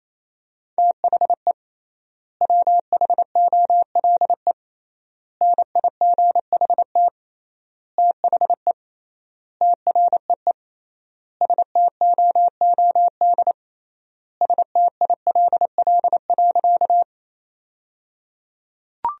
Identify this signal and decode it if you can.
Morse code, heard as a keyed tone, about 28 wpm, 700 Hz. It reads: THE WHOLE NIGHT THE TREE STOOD STILL.
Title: 3x Word-Spacing - Sentences from Top 500 Words - 28wpm